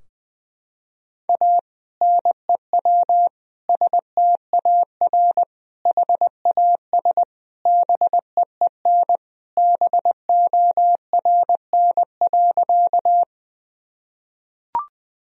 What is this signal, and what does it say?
Morse code, heard as a keyed tone, about 21 wpm, 700 Hz. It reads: A NEW STAR HAS BEEN BORN.
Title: Sentences from Top 1000 Words - Review Part 2 - 20wpm